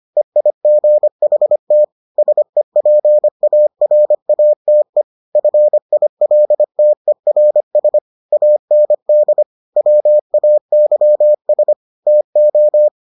Morse code, heard as a keyed tone, about 25 wpm, 600 Hz. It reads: EIGHT SEPARATE FILTERS AND WAYS TO